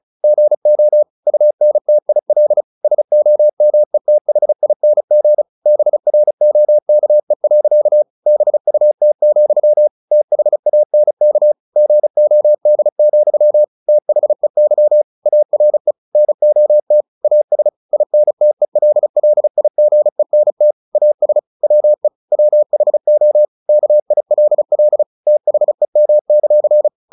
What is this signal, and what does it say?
Morse code, heard as a keyed tone, about 34 wpm, 600 Hz. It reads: GO UNTIL SOMETHING BROKE. BUT, THANK GOD, THEY ARE NOT AS INTELLIGENT AS WE WHO KILL THEM;
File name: the old man and the sea / the old man and the sea 1 35wpm